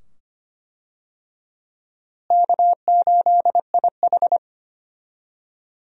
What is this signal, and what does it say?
Morse code, heard as a keyed tone, about 25 wpm, 700 Hz. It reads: K8IH
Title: USA Call Signs - 4 Characters - Repeated 2x - 25wpm